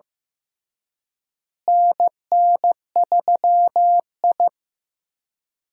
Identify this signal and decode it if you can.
Morse code, heard as a keyed tone, about 15 wpm, 700 Hz. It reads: NN3I
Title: USA Call Signs - 4 Characters - Repeated 3x - 15wpm